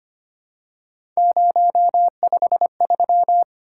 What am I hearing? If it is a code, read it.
Morse code, heard as a keyed tone, about 25 wpm, 700 Hz. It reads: 053
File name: Three Letter-Number Combinations - 25wpm